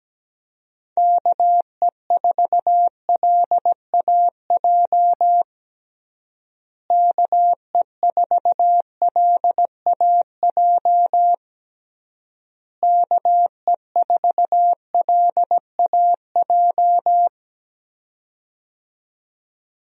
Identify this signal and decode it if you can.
Morse code, heard as a keyed tone, about 17 wpm, 700 Hz. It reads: KE4LAJ KE4LAJ KE4LAJ
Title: USA Call Signs - 6 Characters - Repeated 3x - 17wpm